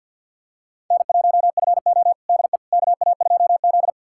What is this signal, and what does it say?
Morse code, heard as a keyed tone, about 46 wpm, 700 Hz. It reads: N1FO DE KA2Z
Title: USA Callsign DE Callsign - 4 Characters - 50wpm